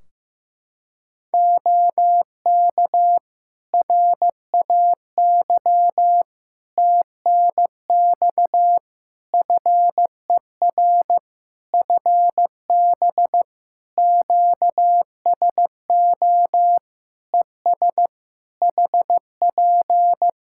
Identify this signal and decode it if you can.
Morse code, heard as a keyed tone, about 15 wpm, 700 Hz. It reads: OK RAY TNX FER FB QSO ES HP
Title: Practice QSO - Exchange 4 - Ending - 15wpm